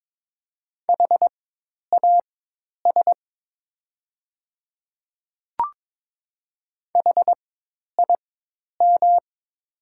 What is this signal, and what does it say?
Morse code, heard as a keyed tone, about 22 wpm, spaced as for 10 wpm, 700 Hz. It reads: HAS HIM
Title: 4x Character Spacing - Top 100 Words - 22wpm